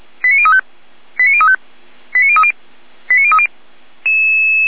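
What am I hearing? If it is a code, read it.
Five-tone selective call (ZVEI), heard as quick runs of five tones, tones of about 70 ms. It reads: 89035 89035 89030 89030
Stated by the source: Five-tone ZVEI selective call received by SDR.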